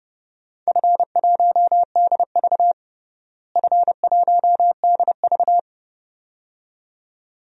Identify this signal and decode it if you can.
Morse code, heard as a keyed tone, about 30 wpm, 700 Hz. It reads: F1DV F1DV
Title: DXCC-100 Call Signs - 4 Characters - Repeated 2x - 30wpm